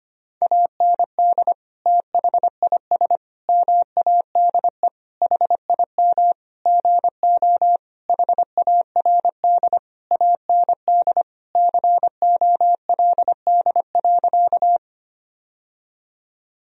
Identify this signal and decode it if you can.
Morse code, heard as a keyed tone, about 25 wpm, 700 Hz. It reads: AND THIS MADE HIM GO HARD AND COLD.
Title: Sentences from Top 1000 Words - Review Part 1 - 25wpm